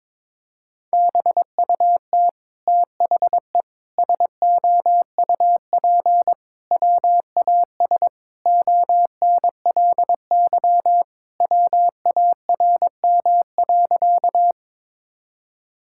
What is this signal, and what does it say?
Morse code, heard as a keyed tone, about 22 wpm, 700 Hz. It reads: BUT THE SOUP WAS ONLY WARM.
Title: Sentences from Top 1000 Words - Review Part 6 - 22wpm